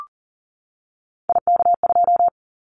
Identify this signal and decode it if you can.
Morse code, heard as a keyed tone, about 40 wpm, 700 Hz. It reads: IK2
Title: Callsign Prefixes for Top 25 DXCC - 40wpm